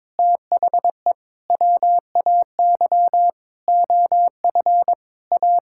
Morse code, heard as a keyed tone, about 22 wpm, 700 Hz. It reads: THE WAY OF A
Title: Sentences from Top 1000 Words - Review Part 6 - 22wpm